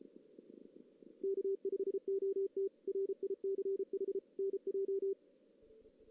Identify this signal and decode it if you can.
Morse code, heard as a keyed tone, about 33 wpm, 390 Hz. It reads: K5OT RICH NJ
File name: 2016 JAN CW NAQP / 2016 Jan naqp k5ot - 0-2